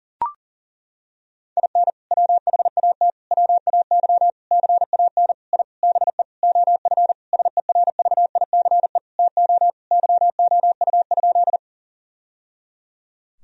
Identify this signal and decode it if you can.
Morse code, heard as a keyed tone, about 37 wpm, 700 Hz. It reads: IN WHAT WAY CAN I BE OF SERVICE TO YOU?
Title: Sentences from Top 1000 Words - Review Part 4 - 40wpm